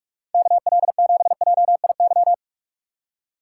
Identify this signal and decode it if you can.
Morse code, heard as a keyed tone, about 46 wpm, 700 Hz. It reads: KL7JIY
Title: USA Call Signs - 6 Characters - 3x Character Spacing - 45wpm